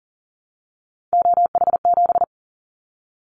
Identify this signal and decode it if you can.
Morse code, heard as a keyed tone, about 38 wpm, 700 Hz. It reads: OH7
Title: Callsign Prefixes for Top 50 DXCC - 40wpm